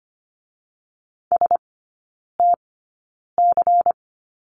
Morse code, heard as a keyed tone, about 25 wpm, spaced as for 9 wpm, 700 Hz. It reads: STC
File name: course / 25:10wpm / 093 Lesson 16e - Letter c - Three Characters 25wpm